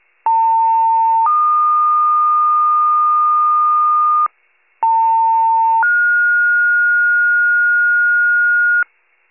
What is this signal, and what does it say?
Two-tone page, heard as a one-second tone, then a three-second tone, about 900 then 1220 Hz, about 900 then 1430 Hz, with noise. It is heard from 0.3 to 8.8 s.
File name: wcalert